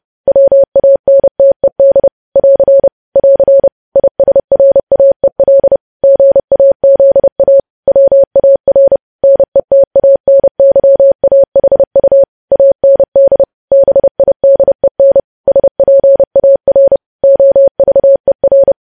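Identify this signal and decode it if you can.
Morse code, heard as a keyed tone, about 30 wpm, 550 Hz. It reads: WANTED <AR> <AR> ISRAEL GAZA WAR NETANYAHU AND BIDEN SPAR OVER
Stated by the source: Morse code news headlines sent at 30, preceded by vvv vvv.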